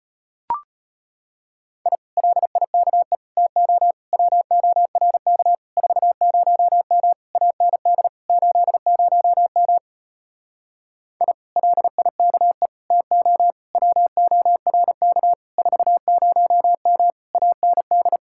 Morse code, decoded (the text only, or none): I LIKE TO WORK 40M AND 80M I LIKE TO WORK 40M AND